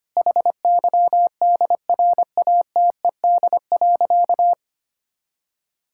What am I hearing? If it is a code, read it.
Morse code, heard as a keyed tone, about 25 wpm, 700 Hz. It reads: HYDRATED.
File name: practice / 25wpm / Difficult Sentences with Hints - 25wpm